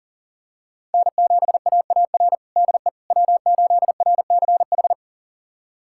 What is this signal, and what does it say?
Morse code, heard as a keyed tone, about 40 wpm, 700 Hz. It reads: N7AAR DE W8RCH
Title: USA Callsign DE Callsign - 5 Characters - 40wpm